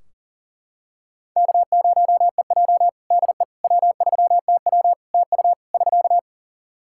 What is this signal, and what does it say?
Morse code, heard as a keyed tone, about 40 wpm, 700 Hz. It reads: K0EJ DE W3TW TU <SK>
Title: Practice QSO - Exchange 4 - Ending - 40wpm